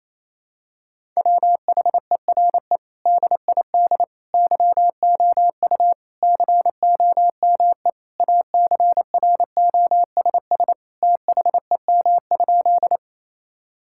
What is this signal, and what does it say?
Morse code, heard as a keyed tone, about 28 wpm, 700 Hz. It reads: WHERE DID YOU COME ACROSS THEM?
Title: Sentences from Top 1000 Words - Review Part 5 - 28wpm